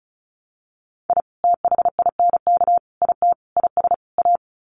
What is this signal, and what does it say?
Morse code, heard as a keyed tone, about 34 wpm, 700 Hz. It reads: I THINK IT IS A
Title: Sets of 5 Words Encore 35wpm